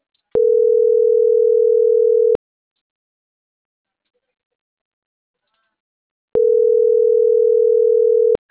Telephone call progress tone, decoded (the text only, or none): ringback tone